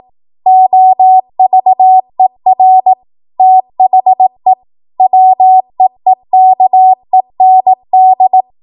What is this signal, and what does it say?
Morse code, heard as a keyed tone, about 18 wpm, 750 Hz. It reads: OVER THE WEEKEND
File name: Archive / 18 WPM / 18WPM